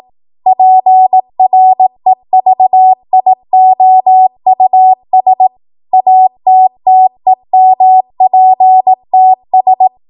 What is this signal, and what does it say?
Morse code, heard as a keyed tone, about 18 wpm, 750 Hz. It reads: PREVIOUS ATTEMPTS
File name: Morse / 18WPM